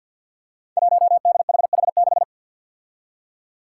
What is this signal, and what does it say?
Morse code, heard as a keyed tone, about 51 wpm, 700 Hz. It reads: 1DHH6
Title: Five Challenging Letter-Number Combinations 50wpm